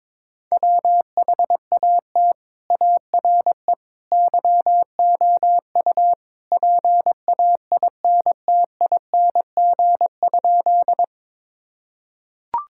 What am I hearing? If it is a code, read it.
Morse code, heard as a keyed tone, about 21 wpm, 700 Hz. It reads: WHAT ARE YOU PAINTING?
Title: Sentences from Top 1000 Words - Review Part 6 - 22wpm